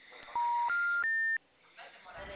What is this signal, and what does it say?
Special information tone, with speech.